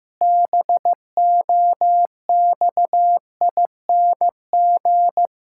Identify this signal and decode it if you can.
Morse code, heard as a keyed tone, about 15 wpm, 700 Hz. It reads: BOXING
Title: Pangram Challenge v2 - 15wpm